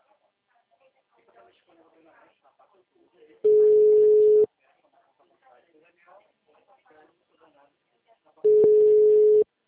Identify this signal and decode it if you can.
Ringback tone (Europe, 425 Hz), with speech.